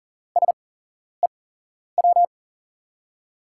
Morse code, heard as a keyed tone, about 39 wpm, spaced as for 11 wpm, 700 Hz. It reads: SEW
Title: USA Call Signs - 5 Characters - 8x Character Spacing - 40wpm